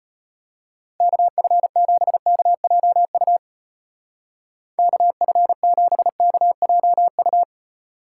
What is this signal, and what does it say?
Morse code, heard as a keyed tone, about 31 wpm, 700 Hz. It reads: KF7KJU KF7KJU
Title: Speed Racing - USA Call Signs 25wpm